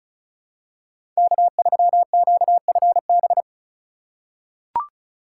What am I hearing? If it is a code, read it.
Morse code, heard as a keyed tone, about 34 wpm, 700 Hz. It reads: K3QFB
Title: USA Call Signs - 5 Characters - 3x Character Spacing - 35wpm